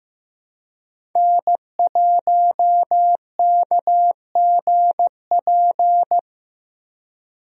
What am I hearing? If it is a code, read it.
Morse code, heard as a keyed tone, about 15 wpm, 700 Hz. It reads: N1KGP
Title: USA Call Signs - 5 Characters - Repeated 3x - 15wpm